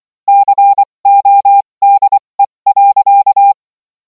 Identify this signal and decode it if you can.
Morse code, heard as a keyed tone, about 24 wpm, spaced as for 20 wpm, 800 Hz. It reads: CODE.